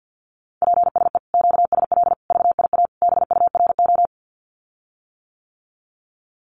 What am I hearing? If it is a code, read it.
Morse code, heard as a keyed tone, about 50 wpm, 700 Hz. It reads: PSE QSL VIA BURO
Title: Common QSO Phrases Encore 50wpm